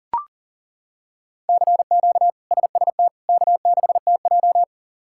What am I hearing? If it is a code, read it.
Morse code, heard as a keyed tone, about 40 wpm, 700 Hz. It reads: CQ SST K6TJ
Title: K1USN SST QSO Practice - 40wpm